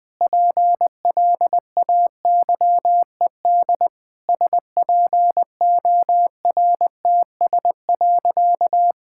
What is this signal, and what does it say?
Morse code, heard as a keyed tone, about 20 wpm, 700 Hz. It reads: PLAYED SPORTS.
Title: Sentences with 6-Word Phrases - Spoken Beginning - 20wpm